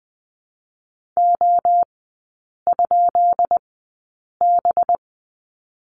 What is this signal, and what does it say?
Morse code, heard as a keyed tone, about 20 wpm, 700 Hz. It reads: O ? B
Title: 184 Lesson 30e - Letter b - Three Characters 20wpm